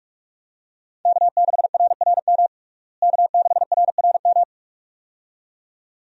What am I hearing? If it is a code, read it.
Morse code, heard as a keyed tone, about 46 wpm, 700 Hz. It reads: K6RRM K6RRM
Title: USA Call Signs - 5 Characters - Repeated 2x - 45wpm